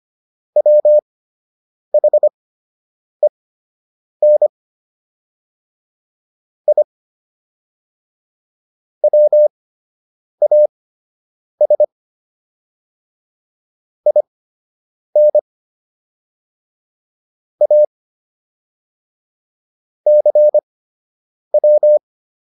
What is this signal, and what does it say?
Morse code, heard as a keyed tone, about 25 wpm, spaced as for 8 wpm, 600 Hz. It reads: WHEN I WAS IN A CW